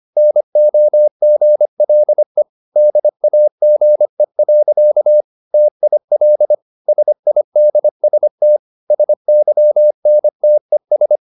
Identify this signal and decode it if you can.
Morse code, heard as a keyed tone, about 25 wpm, 600 Hz. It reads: NOGLE DAGE. TIL SIDST SYNTES
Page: Ferie 25 wpm